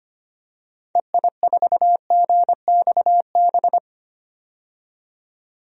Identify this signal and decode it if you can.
Morse code, heard as a keyed tone, about 25 wpm, 700 Hz. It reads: EI4GXB
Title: DXCC-100 Call Signs - 6 Characters - 25wpm